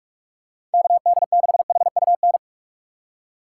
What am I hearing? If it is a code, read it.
Morse code, heard as a keyed tone, about 46 wpm, 700 Hz. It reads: KD6HUN
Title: USA Call Signs - 6 Characters - 3x Character Spacing - 45wpm